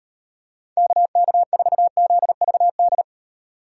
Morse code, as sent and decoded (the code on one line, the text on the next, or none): -.- -.- ....- --.. ...- -..
KK4ZVD